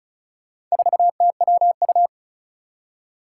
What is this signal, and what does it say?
Morse code, heard as a keyed tone, about 36 wpm, 700 Hz. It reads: 4TWU
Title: Four Challenging Letter-Number Combinations 35wpm